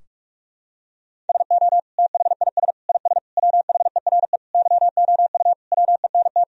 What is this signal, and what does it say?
Morse code, heard as a keyed tone, about 46 wpm, 700 Hz. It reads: SO THIS IS WHERE YOU WENT
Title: Sentences from Top 1000 Words - Review Part 2 - 45wpm